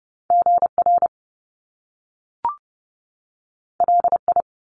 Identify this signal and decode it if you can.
Morse code, heard as a keyed tone, about 30 wpm, 700 Hz. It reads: GR LI
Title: Top Two Letter Combinations in Words 30wpm